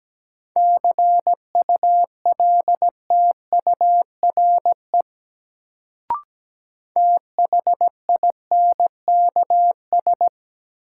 Morse code, heard as a keyed tone, about 17 wpm, 700 Hz. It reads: CULTURE THINKS
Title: Sentences from Top 1000 Words - Review Part 6 - New Word Drill - 17wpm